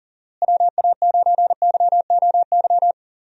Morse code, heard as a keyed tone, about 39 wpm, 700 Hz. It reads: WA9YOY
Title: USA Call Signs - 6 Characters - Repeated 2x - 40wpm